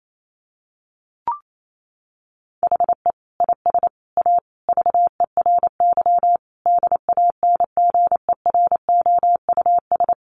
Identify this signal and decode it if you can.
Morse code, heard as a keyed tone, about 28 wpm, 700 Hz. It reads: HE IS A VERY DANGEROUS
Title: Sentences from Top 1200 Words 28wpm